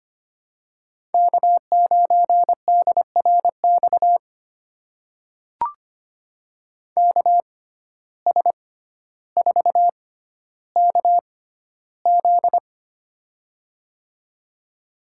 Morse code, heard as a keyed tone, about 25 wpm, spaced as for 13 wpm, 700 Hz. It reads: K9DRX KS4KZ
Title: USA Call Signs - 6x Character Spacing - 25wpm